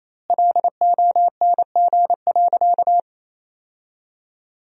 Morse code, heard as a keyed tone, about 28 wpm, 700 Hz. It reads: LONG.
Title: 2x Word-Spacing - Sentences from Top 500 Words - 28wpm